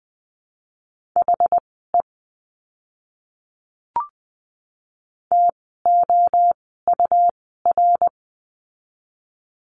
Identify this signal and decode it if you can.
Morse code, heard as a keyed tone, about 20 wpm, spaced as for 14 wpm, 700 Hz. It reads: HE TOUR
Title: 101 Lesson 17f - Letter u - Words 20wpm